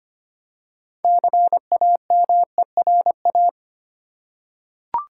Morse code, heard as a keyed tone, about 25 wpm, 700 Hz. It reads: CAMERA
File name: practice / 25wpm / Easy Words with Hints - 25wpm